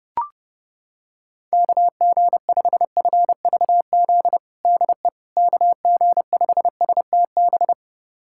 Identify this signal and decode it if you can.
Morse code, heard as a keyed tone, about 32 wpm, 700 Hz. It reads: KG5FVZ DE KG5STB
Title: USA Callsign DE Callsign - 6 Characters - 30wpm